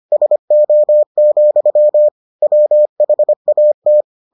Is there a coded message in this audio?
Yes: Morse code from 0.1 to 4.0 s, heard as a keyed tone, about 25 wpm, 600 Hz. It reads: SO, WHAT